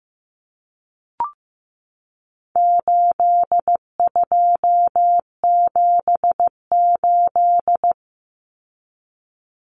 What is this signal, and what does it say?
Morse code, heard as a keyed tone, about 15 wpm, 700 Hz. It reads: 8278
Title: Numbers - 4-Digit 15wpm